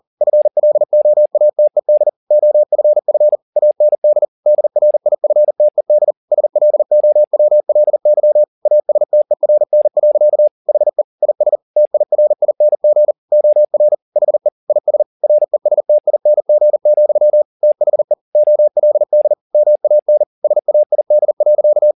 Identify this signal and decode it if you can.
Morse code, heard as a keyed tone, about 40 wpm, 600 Hz. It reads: FLOATED OFF AND DRIFTED SLOWLY ASTERN. HE IS TIRING OR HE IS RESTING, THE OLD MAN SAID.
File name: the old man and the sea 2 40wpm